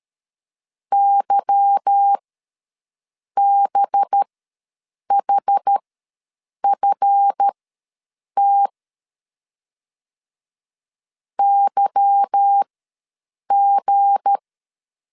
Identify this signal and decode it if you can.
Morse code, heard as a keyed tone, about 13 wpm, spaced as for 7 wpm, 800 Hz. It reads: YBHFT YG
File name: LEZIONI / 8 YJ45 / 8 Y ws 12wpm 3409